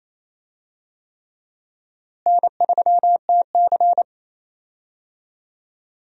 Morse code, heard as a keyed tone, about 28 wpm, 700 Hz. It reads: N3TC
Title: USA Call Signs - 4 Characters - Repeated 3x - 28wpm